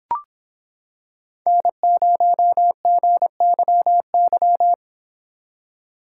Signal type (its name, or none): Morse code